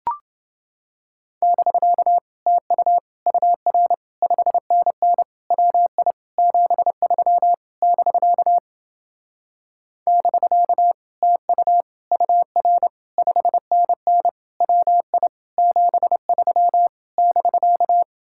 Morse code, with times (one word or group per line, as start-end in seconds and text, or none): 1.4-2.2 <BK>
2.5-3.0 TU
3.3-3.9 UR
4.2-5.2 5NN
5.5-6.1 WI
6.4-7.5 73
7.8-8.6 <BK>
10.1-10.9 <BK>
11.2-11.8 TU
12.1-12.9 UR
13.2-14.3 5NN
14.6-15.3 WI
15.6-16.9 73
17.2-18.0 <BK>